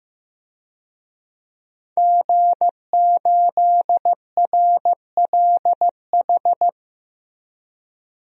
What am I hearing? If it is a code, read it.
Morse code, heard as a keyed tone, about 16 wpm, 700 Hz. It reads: G8RLH
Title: DXCC-100 Call Signs - 5 Characters - 15wpm